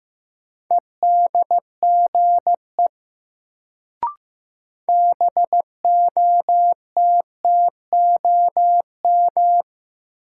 Morse code, heard as a keyed tone, about 15 wpm, 700 Hz. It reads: EDGE BOTTOM
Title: Sets of 3 Words - New Word Drill - 15wpm